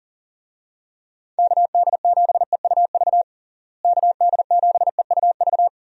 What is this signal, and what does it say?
Morse code, heard as a keyed tone, about 40 wpm, 700 Hz. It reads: KD7EUV KD7EUV
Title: USA Call Signs - 6 Characters - Repeated 3x - 40wpm